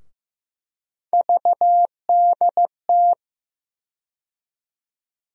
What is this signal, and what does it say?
Morse code, heard as a keyed tone, about 15 wpm, 700 Hz. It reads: VDT
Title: USA Call Sign Postfixes - 3 Letters 15wpm